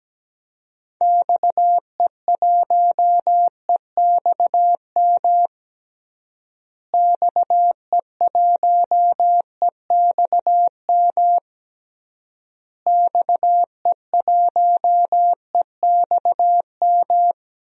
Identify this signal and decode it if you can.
Morse code, heard as a keyed tone, about 17 wpm, 700 Hz. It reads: XE1EXM XE1EXM XE1EXM
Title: DXCC-100 Call Signs - 6 Characters - Repeated 3x - 17wpm